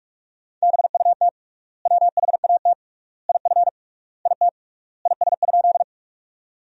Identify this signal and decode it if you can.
Morse code, heard as a keyed tone, about 46 wpm, 700 Hz. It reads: BUT WHAT IF IT IS?
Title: 3x Word-Spacing - Sentences from Top 100 Words - 45wpm